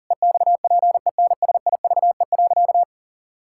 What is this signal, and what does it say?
Morse code, heard as a keyed tone, about 37 wpm, 700 Hz. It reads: EXPENSIVE.
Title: Sentences from Top 1000 Words - Review Part 6 - 40wpm